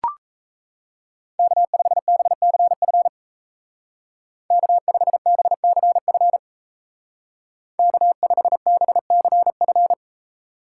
Morse code, heard as a keyed tone, about 34 wpm, 700 Hz. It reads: K5BCF K5BCF K5BCF
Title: Speed Racing - USA Call Signs 28wpm